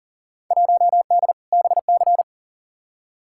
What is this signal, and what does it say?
Morse code, heard as a keyed tone, about 40 wpm, 700 Hz. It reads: 1D BC
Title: Field Day Exchanges - 40wpm